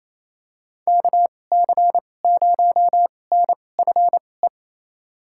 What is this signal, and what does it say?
Morse code, heard as a keyed tone, about 28 wpm, spaced as for 20 wpm, 690 Hz. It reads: KC0NFE
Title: USA Call Signs - 6 Characters - 2x Character Spacing - 28wpm